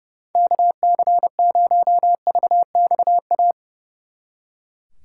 Morse code, heard as a keyed tone, about 37 wpm, 700 Hz. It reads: KC0VXA
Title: Speed Racing - USA 2x3 Call Signs 30wpm